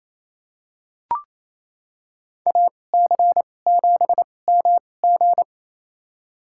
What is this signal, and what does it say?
Morse code, heard as a keyed tone, about 28 wpm, spaced as for 20 wpm, 700 Hz. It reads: AC7MG
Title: USA Call Signs - 5 Characters - 2x Character Spacing - 28wpm